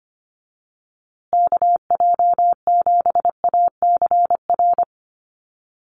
Morse code, heard as a keyed tone, about 25 wpm, 700 Hz. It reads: KJ7ACR
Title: USA Call Signs Repeated 2x 25wpm